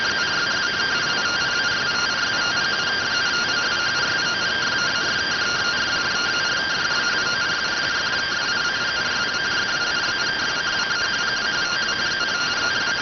Two-tone teleprinter signal, about 170 Hz shift.